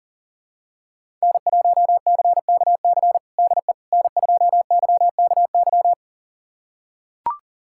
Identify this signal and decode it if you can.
Morse code, heard as a keyed tone, about 40 wpm, 700 Hz. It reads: N1CKC DE N2YKY
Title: USA Callsign DE Callsign - 5 Characters - 40wpm